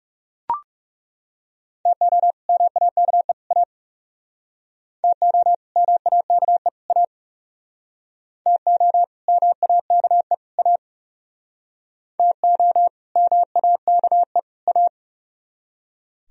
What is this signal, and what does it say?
Morse code, heard as a keyed tone, about 37 wpm, 700 Hz. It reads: TO MAKE A TO MAKE A TO MAKE A TO MAKE A
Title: Speed-Racing - Sets of 3 Words 30wpm